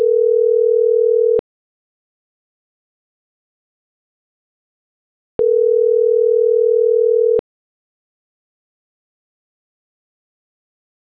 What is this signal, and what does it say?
Ringback tone (North America), 440 + 480 Hz.